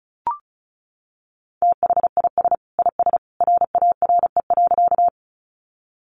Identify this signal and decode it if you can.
Morse code, heard as a keyed tone, about 34 wpm, 700 Hz. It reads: THIS IS RARE.
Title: Sentences from Top 1100 Words 35wpm